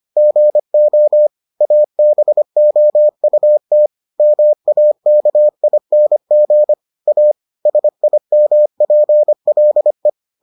Morse code, heard as a keyed tone, about 25 wpm, 600 Hz. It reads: GO ABOUT MAKING A SIMPLE